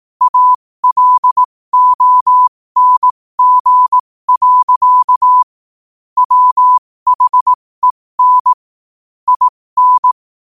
Morse code, heard as a keyed tone, about 18 wpm, spaced as for 15 wpm, 1000 Hz. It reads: ALONG. WHEN IN
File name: qotdfiles / 15wpm